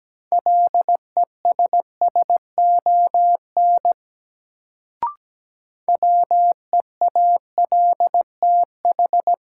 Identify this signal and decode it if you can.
Morse code, heard as a keyed tone, about 17 wpm, 700 Hz. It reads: LESSON WEALTH
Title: Sentences from Top 1200 Words - New Word Drill Alt - 17wpm